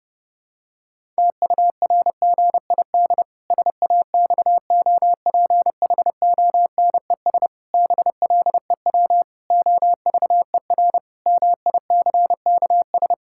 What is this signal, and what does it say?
Morse code, heard as a keyed tone, about 30 wpm, 700 Hz. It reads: TURGID SAXOPHONES BLEW OVER MICKS
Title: Pangram Challenge v2 - 30wpm